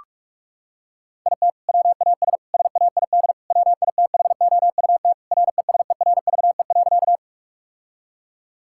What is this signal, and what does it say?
Morse code, heard as a keyed tone, about 46 wpm, 700 Hz. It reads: IT WAS SAID WITHOUT RESERVE.
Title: Sentences from Top 1000 Words - Review Part 6 - 45wpm